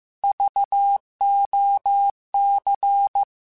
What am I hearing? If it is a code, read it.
Morse code, heard as a keyed tone, about 15 wpm, 800 Hz. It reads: VOC